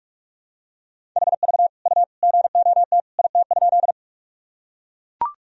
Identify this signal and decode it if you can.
Morse code, heard as a keyed tone, about 43 wpm, 700 Hz. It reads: HV U GOT IT?
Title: Sentences from Top 300 Words and Abbreviations 45wpm